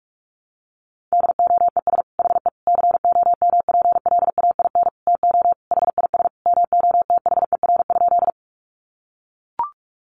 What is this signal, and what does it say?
Morse code, heard as a keyed tone, about 44 wpm, 700 Hz. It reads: DOES HE COMPLAIN TO HIS MOTHER?
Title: Sentences from Top 1300 Words 45wpm